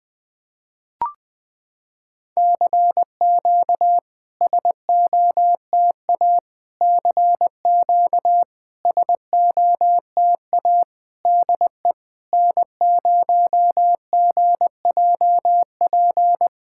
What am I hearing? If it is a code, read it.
Morse code, heard as a keyed tone, about 20 wpm, 700 Hz. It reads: CQ SOTA CQ SOTA DE N0GJP
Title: SOTA QSO Practice 20wpm